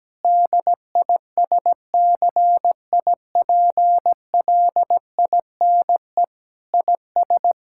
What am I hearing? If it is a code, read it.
Morse code, heard as a keyed tone, about 17 wpm, 700 Hz. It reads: DISCIPLINE IS
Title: Sentences from Top 1000 Words - Review Part 6 - 17wpm